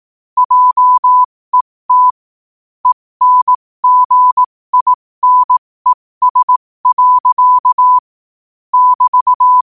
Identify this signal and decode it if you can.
Morse code, heard as a keyed tone, about 18 wpm, spaced as for 15 wpm, 1000 Hz. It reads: JET ENGINES. <BT>
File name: qotdfiles / 15wpm